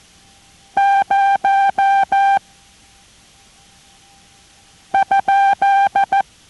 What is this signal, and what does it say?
Morse code, heard as a keyed tone, about 14 wpm, 790 Hz. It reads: 0 ?